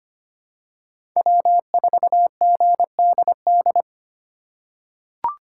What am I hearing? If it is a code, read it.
Morse code, heard as a keyed tone, about 25 wpm, 700 Hz. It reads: W4GDD
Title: USA Call Signs - 4x Character Spacing - 25wpm